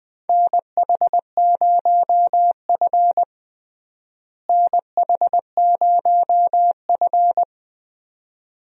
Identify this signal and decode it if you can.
Morse code, heard as a keyed tone, about 20 wpm, 700 Hz. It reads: NH0F NH0F
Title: USA Call Signs - 4 Characters - Repeated 2x - 20wpm